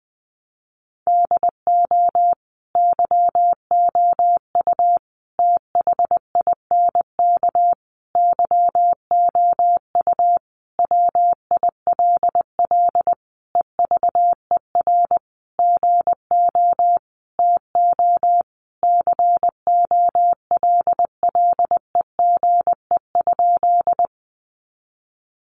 Morse code, heard as a keyed tone, about 20 wpm, 700 Hz. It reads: DO YOU THINK YOU WILL EVER GO TO COLLEGE?
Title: Sentences from Top 1300 Words 20wpm